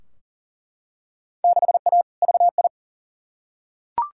Morse code, heard as a keyed tone, about 38 wpm, 700 Hz. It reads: 6A VI
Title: Field Day Exchanges Encore - 40wpm